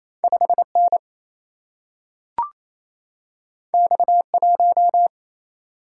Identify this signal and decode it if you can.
Morse code, heard as a keyed tone, about 28 wpm, 700 Hz. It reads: 5N X1
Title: Single-Letter Single-Number 28wpm